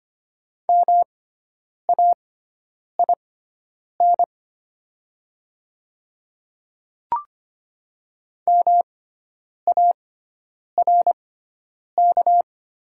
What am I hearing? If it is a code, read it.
Morse code, heard as a keyed tone, about 25 wpm, spaced as for 9 wpm, 700 Hz. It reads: MAIN MARK
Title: 6x Character Spacing - Top 300 Words - 25wpm